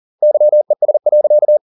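Morse code, heard as a keyed tone, about 40 wpm, 600 Hz. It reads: YES.
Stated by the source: Morse code at 40 wpm, practice text, CW med Gnister.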